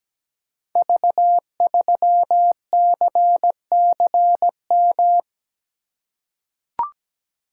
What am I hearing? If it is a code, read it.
Morse code, heard as a keyed tone, about 17 wpm, 700 Hz. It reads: V3CCM
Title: Code Groups - 5 Letter:Numbers - 17wpm